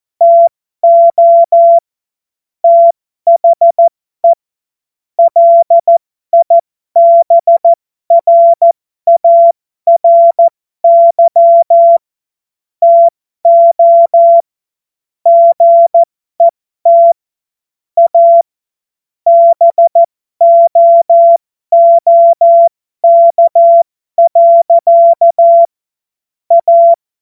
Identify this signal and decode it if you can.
Morse code, heard as a keyed tone, about 14 wpm, spaced as for 12 wpm, 680 Hz. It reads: TO THE LIBRARY TO GET A BOOK. A
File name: Morsekurs / Witze